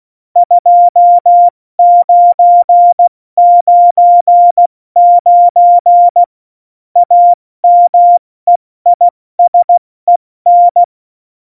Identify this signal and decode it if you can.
Morse code, heard as a keyed tone, about 16 wpm, 700 Hz. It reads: 2999 AMEISEN
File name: Morsekurs / Witze / Witz 03.Juli 201, 70 F80